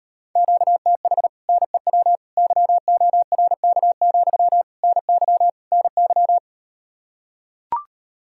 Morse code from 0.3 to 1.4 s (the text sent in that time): QTH